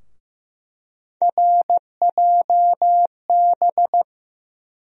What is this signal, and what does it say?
Morse code, heard as a keyed tone, about 15 wpm, 700 Hz. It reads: RJB
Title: USA Call Sign Postfixes - 3 Letters 15wpm